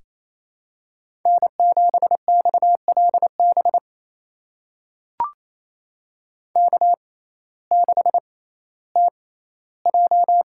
Morse code, heard as a keyed tone, about 28 wpm, spaced as for 15 wpm, 700 Hz. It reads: N7XLB K6TJ
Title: USA Call Signs - 5 Characters - 6x Character Spacing - 28wpm